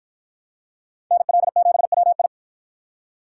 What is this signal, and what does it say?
Morse code, heard as a keyed tone, about 53 wpm, 700 Hz. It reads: NL7WI